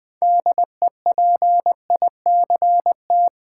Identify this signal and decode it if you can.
Morse code, heard as a keyed tone, about 20 wpm, 700 Hz. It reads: DEPICT
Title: 6 Letter Words - First 3 Letters Given - 20wpm